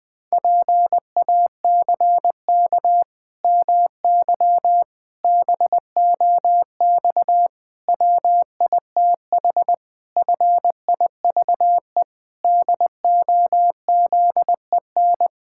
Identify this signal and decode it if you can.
Morse code, heard as a keyed tone, about 19 wpm, 700 Hz. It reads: PACK MY BOX WITH FIVE DOZEN
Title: Pangram Challenge v2 - 20wpm